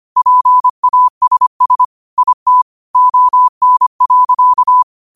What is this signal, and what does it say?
Morse code, heard as a keyed tone, about 25 wpm, 1000 Hz. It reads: PASS IT ON.